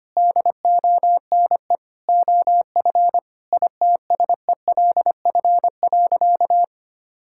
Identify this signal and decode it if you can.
Morse code, heard as a keyed tone, about 25 wpm, 700 Hz. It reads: DONE OF ITSELF.
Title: Sentences from Top 1000 Words - Review Part 5 - 25wpm